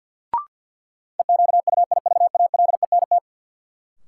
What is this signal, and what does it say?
Morse code, heard as a keyed tone, about 53 wpm, 700 Hz. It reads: EQUIVALENT
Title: Sentences from Top 1600 Words - New Word Drill - 50wpm